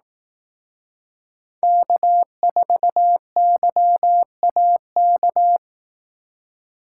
Morse code, heard as a keyed tone, about 18 wpm, 700 Hz. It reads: K4YAK
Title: Speed Racing - USA 1x3 Call Signs 15wpm